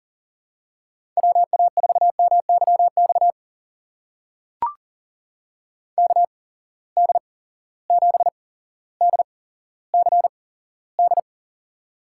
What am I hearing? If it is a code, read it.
Morse code, heard as a keyed tone, about 39 wpm, spaced as for 17 wpm, 700 Hz. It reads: WA4MYX KD7DCD
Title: USA Call Signs - 6 Characters - 8x Character Spacing - 40wpm